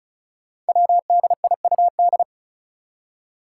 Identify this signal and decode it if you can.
Morse code, heard as a keyed tone, about 36 wpm, 700 Hz. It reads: WDIUD
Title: Code Groups - 5 Letter:Numbers - 35wpm